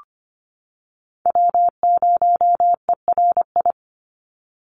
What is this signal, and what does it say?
Morse code, heard as a keyed tone, about 25 wpm, 700 Hz. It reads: W0ERI
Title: NA SOTA Chaser Callsigns 25wpm